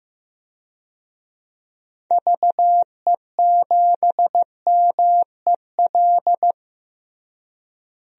Morse code, heard as a keyed tone, about 15 wpm, 700 Hz. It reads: VE7MEL
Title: DXCC-100 Call Signs - 6 Characters - Repeated 3x - 15wpm